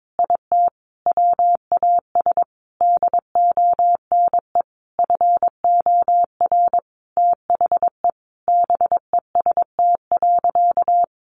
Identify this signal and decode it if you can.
Morse code, heard as a keyed tone, about 22 wpm, 700 Hz. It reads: IT WAS DONE FOR THE BEST.
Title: Sentences from Top 600 Words 22wpm